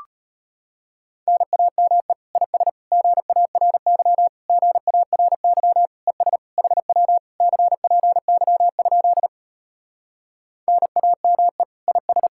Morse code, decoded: NAME IS GARY GARY ES HW CPY? NAME IS

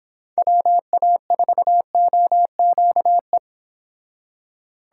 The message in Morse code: WA4OQE

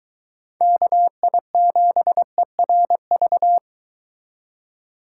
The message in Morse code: KI7ERV

